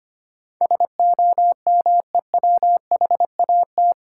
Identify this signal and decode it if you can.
Morse code, heard as a keyed tone, about 25 wpm, 700 Hz. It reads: SOMEWHAT
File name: 8 Letter Words - First 3 Letters Given - 25wpm